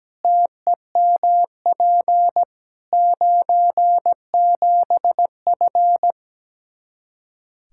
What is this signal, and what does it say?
Morse code, heard as a keyed tone, about 17 wpm, 700 Hz. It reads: TEMP 97F
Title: Practice QSO - Exchange 2 - Info - 17wpm